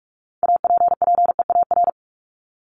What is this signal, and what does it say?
Morse code, heard as a keyed tone, about 44 wpm, 700 Hz. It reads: APPEAR
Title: ES:S Suffix - Top 100 Words with Base Words 45wpm